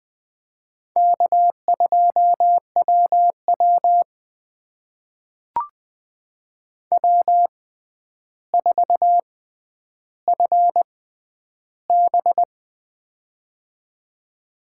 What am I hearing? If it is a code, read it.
Morse code, heard as a keyed tone, about 20 wpm, spaced as for 10 wpm, 700 Hz. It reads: K2WW W4FB
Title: USA Call Signs - 4 Characters - 6x Character Spacing - 20wpm